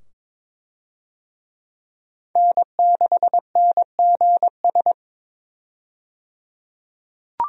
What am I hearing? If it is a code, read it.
Morse code, heard as a keyed tone, about 22 wpm, 700 Hz. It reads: N6NGS
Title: USA Call Signs - 5 Characters - 22wpm